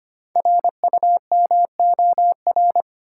Morse code, heard as a keyed tone, about 25 wpm, 700 Hz. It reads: RUMOR
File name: 5 Letter Words - First 3 Letters Given - 25wpm